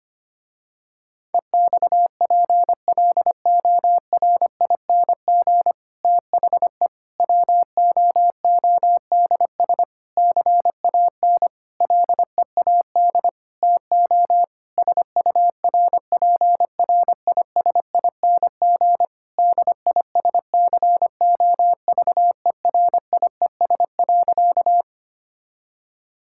Morse code, heard as a keyed tone, about 25 wpm, 700 Hz. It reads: EXPLORING THE WOODS CAN LEAD TO SURPRISING DISCOVERIES.
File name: practice / 25wpm / Difficult Sentences with Hints - 25wpm